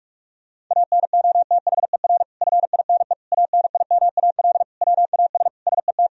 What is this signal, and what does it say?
Morse code, heard as a keyed tone, about 41 wpm, 700 Hz. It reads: ANOTHER FINE ANIMAL WAS SET
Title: Sentences from Top 1000 Words - Review Part 1 - 45wpm